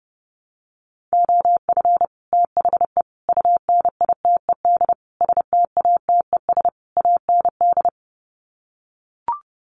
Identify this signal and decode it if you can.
Morse code, heard as a keyed tone, about 30 wpm, 700 Hz. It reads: OF THE UNITED STATES AND
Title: Sets of 5 Words 30wpm